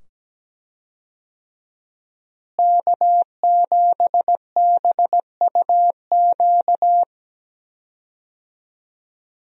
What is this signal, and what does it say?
Morse code, heard as a keyed tone, about 17 wpm, 700 Hz. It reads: K7BUQ
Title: USA Call Signs - 5 Characters - Repeated 2x - 17wpm